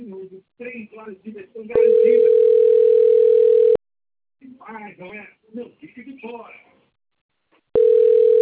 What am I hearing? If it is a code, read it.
Ringback tone (North America), 440 + 480 Hz, with speech.
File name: rg-1600-unknown- - - .9470